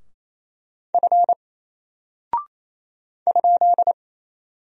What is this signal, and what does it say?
Morse code, heard as a keyed tone, about 27 wpm, 700 Hz. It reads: F ?
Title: Single Character (Letter:Number:Punctuation:Prosign) 28wpm